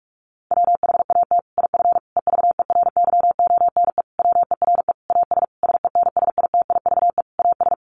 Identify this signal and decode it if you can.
Morse code, heard as a keyed tone, about 44 wpm, 700 Hz. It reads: WHAT IF EVERYONE WERE AS SENSITIVE AS
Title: Sentences from Top 1400 Words 45wpm